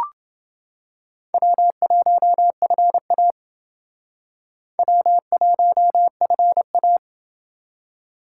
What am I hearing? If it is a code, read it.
Morse code, heard as a keyed tone, about 28 wpm, 700 Hz. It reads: W1FA W1FA